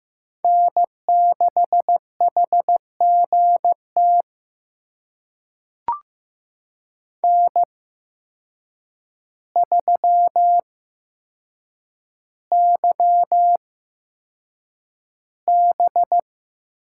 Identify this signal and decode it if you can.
Morse code, heard as a keyed tone, about 15 wpm, spaced as for 7 wpm, 700 Hz. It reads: N6HGT N3YB
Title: USA Call Signs - 5 Characters - 8x Character Spacing - 15wpm